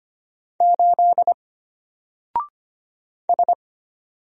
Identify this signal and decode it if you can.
Morse code, heard as a keyed tone, about 25 wpm, 700 Hz. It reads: 8 S
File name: Single-Letter-Number-25wpm